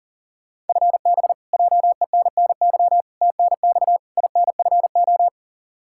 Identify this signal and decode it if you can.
Morse code, heard as a keyed tone, about 40 wpm, 700 Hz. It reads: FB JENNY TNX INFO
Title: K1USN SST QSO Practice 40wpm